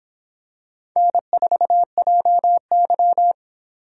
Morse code, heard as a keyed tone, about 26 wpm, 700 Hz. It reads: N4JY